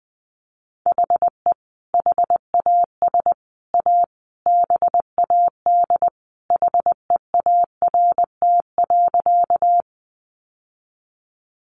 Morse code, heard as a keyed tone, about 20 wpm, 700 Hz. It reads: HE HAS A BAD HEART.
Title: Sentences from Top 1200 Words 20wpm